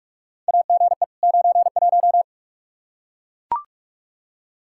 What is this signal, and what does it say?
Morse code, heard as a keyed tone, about 44 wpm, 700 Hz. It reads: AGE 91
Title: Practice QSO - Exchange 3 - More Info - 45wpm